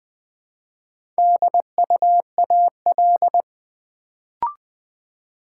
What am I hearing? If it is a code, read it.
Morse code, heard as a keyed tone, about 20 wpm, 700 Hz. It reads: DUAL
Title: Build A Word - 2 Letters At A Time - 4-Letter Words - 20wpm